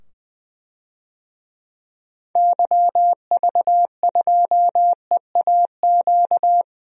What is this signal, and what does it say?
Morse code, heard as a keyed tone, about 21 wpm, 700 Hz. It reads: YV2EAQ
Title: DXCC-100 Call Signs - 6 Characters - 20wpm